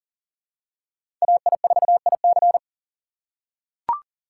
Morse code, heard as a keyed tone, about 40 wpm, 700 Hz. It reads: AI4IC
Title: USA Call Signs - 5 Characters - 2x Character Spacing - 40wpm